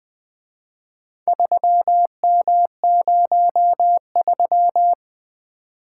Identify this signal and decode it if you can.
Morse code, heard as a keyed tone, about 24 wpm, 700 Hz. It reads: 3M03
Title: Speed-Racing - Four Letter-Number Combinations - 20wpm